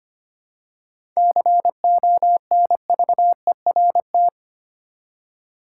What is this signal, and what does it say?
Morse code, heard as a keyed tone, about 25 wpm, 700 Hz. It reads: CONVERT
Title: Seven Letter Words - 25wpm